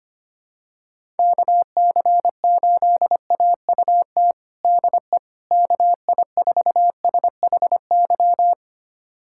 Morse code, heard as a keyed tone, about 25 wpm, 700 Hz. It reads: KC8AUT DE KI4SHY
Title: USA Callsign DE Callsign - 6 Characters - 25wpm